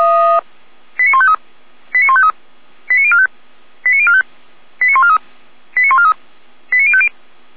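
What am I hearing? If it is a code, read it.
Five-tone selective call (ZVEI), heard as quick runs of five tones, tones of about 70 ms. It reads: 89153 89153 89045 89045 89143 89143 89050
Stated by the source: ZVEI five-tone call received on SDR by feuerwehr-orlamuende.org (fire brigade Orlamünde).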